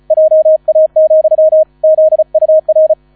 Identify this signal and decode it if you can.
Morse code, heard as a keyed tone, about 36 wpm, 630 Hz. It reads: JA, ZUR